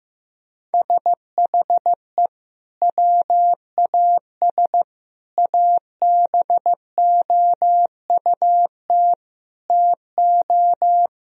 Morse code, heard as a keyed tone, about 15 wpm, 700 Hz. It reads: SHE WAS ABOUT TO
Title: Sentences from Top 1000 Words - Review Part 2 - 15wpm